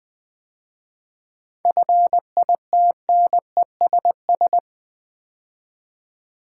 Morse code, heard as a keyed tone, about 20 wpm, 700 Hz. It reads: FITNESS
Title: 7 Letter Words - First 3 Letters Given - 20wpm